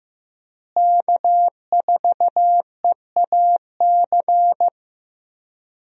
Morse code, heard as a keyed tone, about 15 wpm, 700 Hz. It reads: K4EAC
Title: USA Call Signs - 5 Characters - Repeated 3x - 15wpm